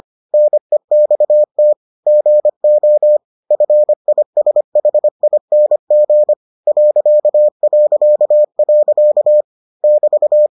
Morse code, heard as a keyed tone, about 25 wpm, 600 Hz. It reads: NEXT GO FISHING ... <BT>